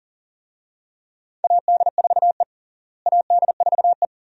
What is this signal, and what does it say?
Morse code, heard as a keyed tone, about 40 wpm, 700 Hz. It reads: AD4E AD4E
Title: USA Call Signs - 4 Characters - Repeated 2x - 40wpm